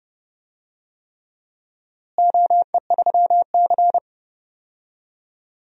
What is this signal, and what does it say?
Morse code, heard as a keyed tone, about 30 wpm, 700 Hz. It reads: OE3C
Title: DXCC-100 Call Signs - 4 Characters - Repeated 3x - 30wpm